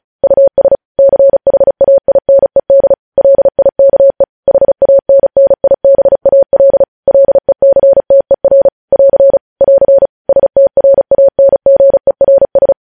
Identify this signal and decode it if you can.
Morse code, heard as a keyed tone, about 36 wpm, 550 Hz. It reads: US CHAINED LIKE HANNIBAL LECTER <AR> <AR> STRANGERS
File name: podcast / News headlines-1x-35x35